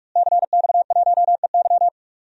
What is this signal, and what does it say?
Morse code, heard as a keyed tone, about 46 wpm, 700 Hz. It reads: CX1EY
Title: DXCC-100 Call Signs - 5 Characters - Repeated 3x - 45wpm